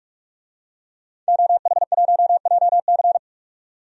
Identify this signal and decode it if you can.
Morse code, heard as a keyed tone, about 46 wpm, 700 Hz. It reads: OH1JC
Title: DXCC-100 Call Signs - 5 Characters - 45wpm